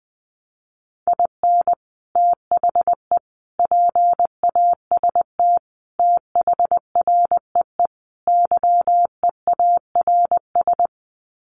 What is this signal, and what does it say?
Morse code, heard as a keyed tone, about 20 wpm, 700 Hz. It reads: IN THE PAST THREE YEARS
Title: Sets of 5 Words Encore2 20wpm